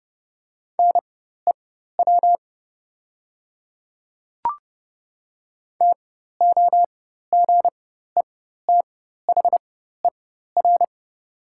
Morse code, heard as a keyed tone, about 30 wpm, spaced as for 14 wpm, 700 Hz. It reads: NEW TOGETHER
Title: 4x Character Spacing - Top 300 Words - 30wpm